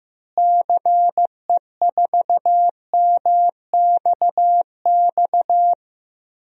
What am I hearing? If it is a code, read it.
Morse code, heard as a keyed tone, about 15 wpm, 700 Hz. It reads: CE4MXX
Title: DXCC-100 Call Signs - 6 Characters - Repeated 2x - 15wpm